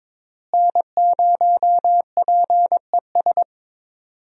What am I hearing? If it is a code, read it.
Morse code, heard as a keyed tone, about 22 wpm, 700 Hz. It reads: N0PES
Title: USA Call Signs - 5 Characters - Repeated 3x - 22wpm